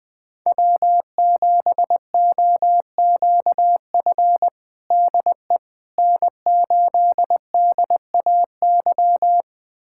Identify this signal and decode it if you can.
Morse code, heard as a keyed tone, about 20 wpm, 700 Hz. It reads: W7OQF DE N8DAY
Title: USA Callsign DE Callsign - 5 Characters - 20wpm